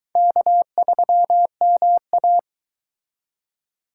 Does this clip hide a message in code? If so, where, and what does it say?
Yes: Morse code from 0.2 to 2.4 s, heard as a keyed tone, about 22 wpm, 700 Hz. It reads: K3MA